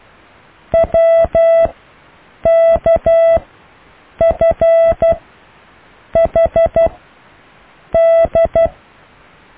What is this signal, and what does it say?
Morse code, heard as a keyed tone, about 12 wpm, spaced as for 7 wpm, 660 Hz. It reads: WKFHD
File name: cw / EKTPV+BFHNW